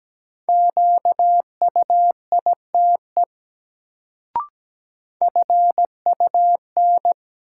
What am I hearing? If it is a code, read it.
Morse code, heard as a keyed tone, about 17 wpm, 700 Hz. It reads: QUITE FUN
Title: Sentences from Top 700 Words - New Word Drill - 17wpm